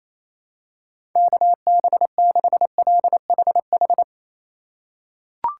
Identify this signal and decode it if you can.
Morse code, heard as a keyed tone, about 28 wpm, 700 Hz. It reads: KB6LHH
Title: USA Call Signs - 6 Characters - 2x Character Spacing - 28wpm